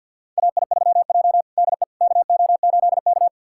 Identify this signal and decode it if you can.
Morse code, heard as a keyed tone, about 50 wpm, 700 Hz. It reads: AI3J DE KO8K
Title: USA Callsign DE Callsign - 4 Characters - 50wpm